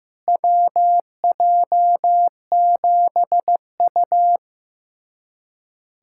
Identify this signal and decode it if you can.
Morse code, heard as a keyed tone, about 16 wpm, 700 Hz. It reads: WJ7U
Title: USA Callsign DE Callsign - 4 Characters - 15wpm